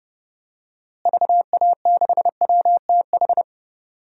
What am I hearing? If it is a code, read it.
Morse code, heard as a keyed tone, about 30 wpm, 700 Hz. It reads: VA6WTH
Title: DXCC-100 Call Signs - 6 Characters - Repeated 2x - 30wpm